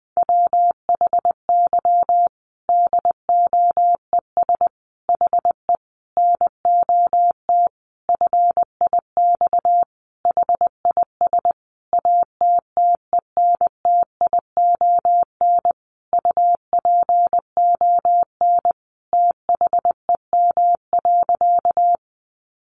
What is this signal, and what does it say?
Morse code, heard as a keyed tone, about 20 wpm, 700 Hz. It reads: WHY DOES HE NOT FIX HIS ATTENTION UPON THEM.
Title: Sentences from Top 1400 Words 20wpm